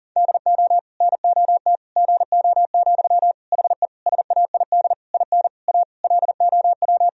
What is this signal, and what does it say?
Morse code, heard as a keyed tone, about 40 wpm, 700 Hz. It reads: DO NOT GO, HE SAID IN A LOW
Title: Sentences from Top 1000 Words - Review Part 1 - 40wpm